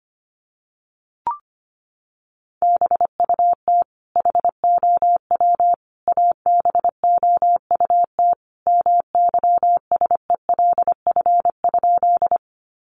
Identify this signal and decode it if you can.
Morse code, heard as a keyed tone, about 25 wpm, 700 Hz. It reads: BUT HOW ABOUT MYSELF?
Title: Sentences from Top 1300 Words 25wpm